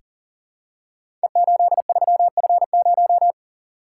Morse code, heard as a keyed tone, about 39 wpm, 700 Hz. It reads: E83F0
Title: Code Groups - 5 Letter:Numbers - 40wpm